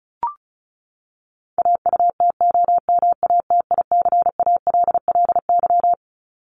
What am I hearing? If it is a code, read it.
Morse code, heard as a keyed tone, about 34 wpm, 700 Hz. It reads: AUTOMATICALLY
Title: Top 1751-2000 Words 35wpm